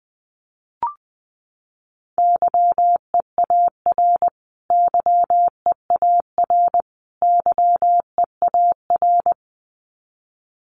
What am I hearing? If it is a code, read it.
Morse code, heard as a keyed tone, about 20 wpm, 700 Hz. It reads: YEAR YEAR YEAR
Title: Top 500 Words Repeated 20wpm